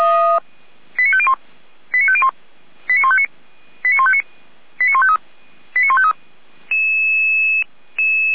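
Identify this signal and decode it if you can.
Five-tone selective call (ZVEI), heard as quick runs of five tones, tones of about 70 ms. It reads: 89501 89501 89159 89159 89153 89153
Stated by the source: ZVEI five-tone call received by SDR, archived on the Feuerwehr Orlamünde audio server (f85360).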